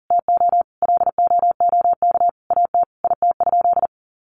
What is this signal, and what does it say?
Morse code, heard as a keyed tone, about 40 wpm, 700 Hz. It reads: TO LOOK AT IT?
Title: Sentences from Top 100 Words 40wpm